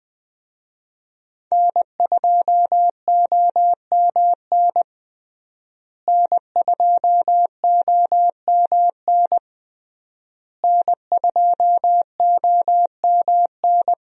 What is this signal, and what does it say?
Morse code, heard as a keyed tone, about 20 wpm, 700 Hz. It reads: N2OMN N2OMN N2OMN
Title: USA Call Signs - 5 Characters - Repeated 3x - 20wpm